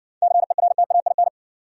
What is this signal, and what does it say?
Morse code, heard as a keyed tone, about 60 wpm, 700 Hz. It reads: CERTAIN